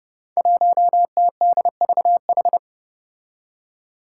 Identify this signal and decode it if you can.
Morse code, heard as a keyed tone, about 30 wpm, 700 Hz. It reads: 1TDVH
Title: Five Challenging Letter-Number Combinations 30wpm